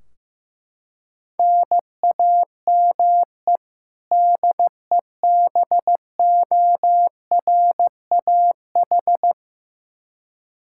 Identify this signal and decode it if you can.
Morse code, heard as a keyed tone, about 15 wpm, 700 Hz. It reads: NAME DEBORAH
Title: Practice QSO - Exchange 1 - Beginning - 15wpm